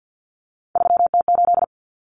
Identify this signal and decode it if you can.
Morse code, heard as a keyed tone, about 48 wpm, 700 Hz. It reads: 3T8
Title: Common Cut Numbers - 3-Digit 50wpm